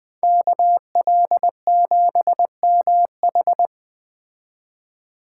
Morse code, heard as a keyed tone, about 20 wpm, 700 Hz. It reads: KL7MH
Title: Speed Racing - USA 2x2 Call Signs 17wpm